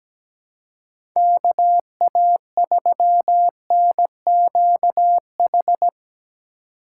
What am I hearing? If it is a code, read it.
Morse code, heard as a keyed tone, about 17 wpm, 700 Hz. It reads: KA3NQH
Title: USA Call Signs - 2x Character Spacing - 17wpm